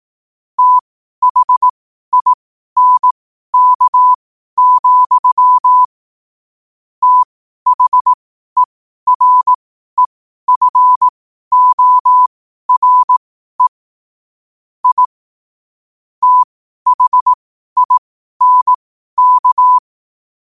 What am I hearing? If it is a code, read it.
Morse code, heard as a keyed tone, about 18 wpm, spaced as for 12 wpm, 1000 Hz. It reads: THINK, THEREFORE I THINK
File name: qotdfiles / 12wpm